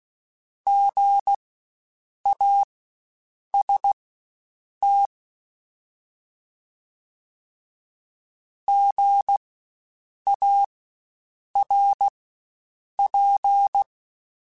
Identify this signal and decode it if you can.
Morse code, heard as a keyed tone, about 16 wpm, spaced as for 7 wpm, 800 Hz. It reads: GAST GARP
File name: Spaced Alphas / Spaced Alphas mp3s / Gs sp @15